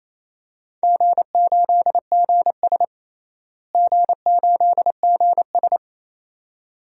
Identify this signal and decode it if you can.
Morse code, heard as a keyed tone, about 28 wpm, 700 Hz. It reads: G8GS G8GS
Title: DXCC-100 Call Signs - 4 Characters - Repeated 2x - 28wpm